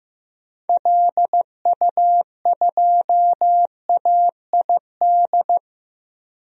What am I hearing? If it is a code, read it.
Morse code, heard as a keyed tone, about 15 wpm, 700 Hz. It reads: LU2AID
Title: DXCC-100 Call Signs - 6 Characters - Repeated 3x - 15wpm